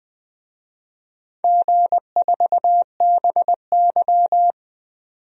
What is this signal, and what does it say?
Morse code, heard as a keyed tone, about 20 wpm, 700 Hz. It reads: G4BY
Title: DXCC-100 Call Signs - 4 Characters - 20wpm